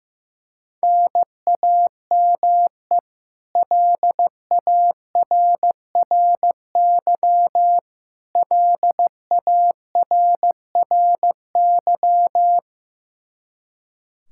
Morse code, heard as a keyed tone, about 15 wpm, 700 Hz. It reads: NAME LARRY LARRY
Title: Practice QSO - Exchange 1 - Beginning - 15wpm